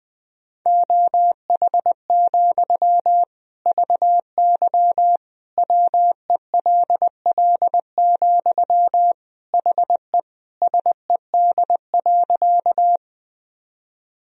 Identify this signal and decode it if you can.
Morse code, heard as a keyed tone, about 20 wpm, 700 Hz. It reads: OH, VY WELL, HE SED.
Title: Sentences from Top 500 Words and Abbreviations 20wpm